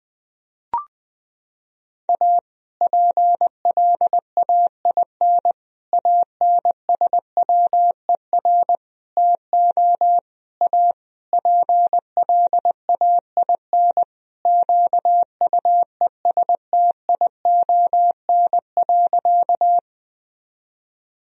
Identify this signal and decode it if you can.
Morse code, heard as a keyed tone, about 21 wpm, 700 Hz. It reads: A PLAIN ANSWER TO A PLAIN QUESTION.
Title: Sentences from Top 1000 Words - Review Part 2 - 20wpm